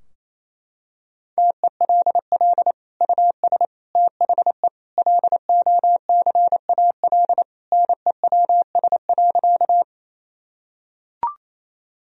Morse code, heard as a keyed tone, about 28 wpm, 700 Hz. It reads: TELL US THE LOCAL NEWS.
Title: Sentences from Top 1000 Words - Review Part 3 - 28wpm